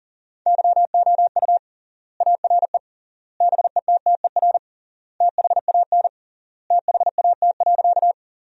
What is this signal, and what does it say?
Morse code, heard as a keyed tone, about 40 wpm, 700 Hz. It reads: YOU ARE BETTER THAN THAT.
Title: 3x Word-Spacing - Sentences from Top 400 Words - 40wpm